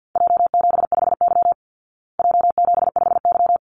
Morse code, heard as a keyed tone, about 49 wpm, 700 Hz. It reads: J75Y J75Y
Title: Call Signs Repeated 3x - 50wpm